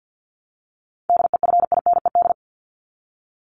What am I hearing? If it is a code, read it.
Morse code, heard as a keyed tone, about 50 wpm, 700 Hz. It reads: DEFINED
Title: ED Suffix - Top 100 Words with Base Words 50wpm